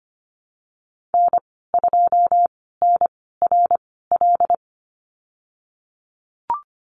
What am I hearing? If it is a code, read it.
Morse code, heard as a keyed tone, about 25 wpm, spaced as for 16 wpm, 700 Hz. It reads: N2NRL
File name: Lesson 15g - Number 5 - Call Signs 25wpm